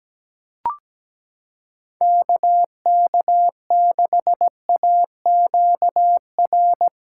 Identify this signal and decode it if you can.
Morse code, heard as a keyed tone, about 17 wpm, 700 Hz. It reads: KK6AQR
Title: USA Callsign DE Callsign 17wpm